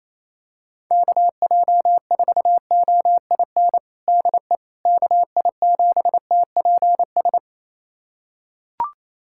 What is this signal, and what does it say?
Morse code, heard as a keyed tone, about 28 wpm, 700 Hz. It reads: KJ4OIN DE KI7TPS
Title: USA Callsign DE Callsign 28wpm